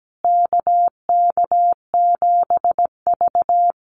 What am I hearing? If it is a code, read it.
Morse code, heard as a keyed tone, about 17 wpm, 700 Hz. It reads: KK7V
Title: USA Call Signs Repeated 2x 17wpm